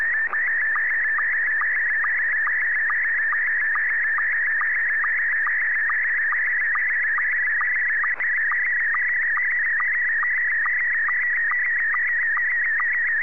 SSTV (slow-scan television), Scottie 1.